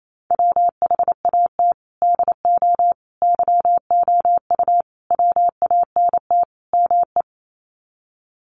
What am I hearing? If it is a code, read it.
Morse code, heard as a keyed tone, about 28 wpm, 700 Hz. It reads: WHAT DO YOU WANT ME
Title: Sets of 5 Words Encore2 28wpm